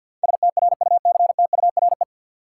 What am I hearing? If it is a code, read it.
Morse code, heard as a keyed tone, about 53 wpm, 700 Hz. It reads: STRUCTURE
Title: Sentences from Top 1000 Words - Review Part 6 - New Word Drill - 50wpm